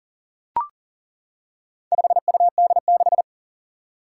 Morse code, heard as a keyed tone, about 41 wpm, 700 Hz. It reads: 5UD6
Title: Four Challenging Letter-Number Combinations 40wpm